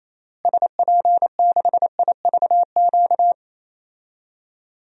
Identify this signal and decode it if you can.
Morse code, heard as a keyed tone, about 28 wpm, 700 Hz. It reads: SP6IVQ
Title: DXCC-100 Call Signs - 6 Characters - Repeated 2x - 28wpm